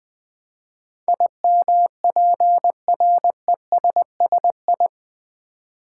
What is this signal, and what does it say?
Morse code, heard as a keyed tone, about 20 wpm, 700 Hz. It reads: IMPRESSI
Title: Build A Word - 4 Letters At A Time - 10-Letter Words - 20wpm